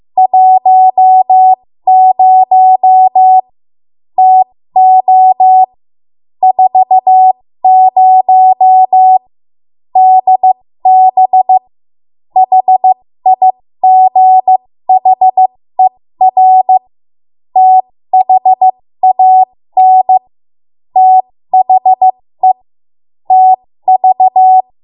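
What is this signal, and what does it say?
Morse code, heard as a keyed tone, about 15 wpm, spaced as for 13 wpm, 750 Hz. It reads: 10 TO 40 DB HIGHER THAN THE TV
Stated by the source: 13 wpm Morse code practice archive transmission.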